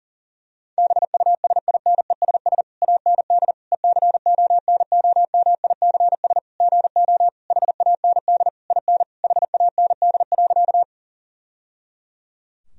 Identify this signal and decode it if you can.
Morse code, heard as a keyed tone, about 37 wpm, 700 Hz. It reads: BUSINESS AND ECONOMICS GO HAND IN HAND.
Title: Sentences from Top 1000 Words - Review Part 3 - 40wpm